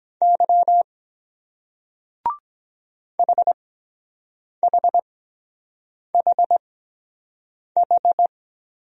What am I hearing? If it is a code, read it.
Morse code, heard as a keyed tone, about 23 wpm, 700 Hz. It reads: Y H H H H